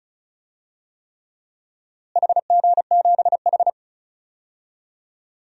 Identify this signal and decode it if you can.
Morse code, heard as a keyed tone, about 36 wpm, 700 Hz. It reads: HG7H
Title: DXCC-100 Call Signs - 4 Characters - Repeated 3x - 35wpm